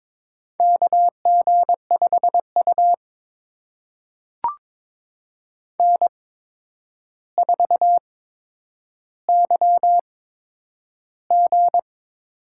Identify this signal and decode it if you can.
Morse code, heard as a keyed tone, about 22 wpm, spaced as for 9 wpm, 700 Hz. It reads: KG5U N4YG
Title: USA Call Signs - 4 Characters - 8x Character Spacing - 22wpm